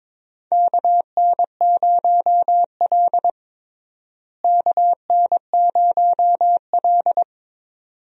Morse code, heard as a keyed tone, about 22 wpm, 700 Hz. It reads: KN0L KN0L
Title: USA Call Signs - 4 Characters - Repeated 2x - 22wpm